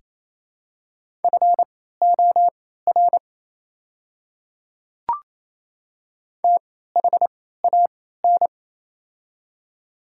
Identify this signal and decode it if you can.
Morse code, heard as a keyed tone, about 27 wpm, spaced as for 16 wpm, 700 Hz. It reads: FOR THAN
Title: 3x Character Spacing - Top 200 Words - 28wpm